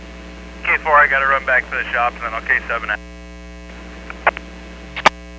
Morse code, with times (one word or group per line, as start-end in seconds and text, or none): none